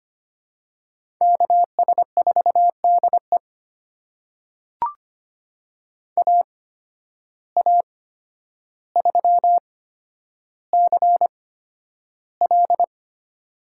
Morse code, heard as a keyed tone, about 25 wpm, spaced as for 11 wpm, 700 Hz. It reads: KS4DE AA3CL
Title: USA Call Signs - 8x Character Spacing - 25wpm